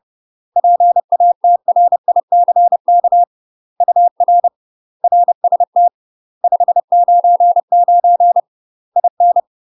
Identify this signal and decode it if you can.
Morse code, heard as a keyed tone, about 30 wpm, 700 Hz. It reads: PATRICK UR RST 599 IN
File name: UK MASTER QSOs TRAINING/30WPM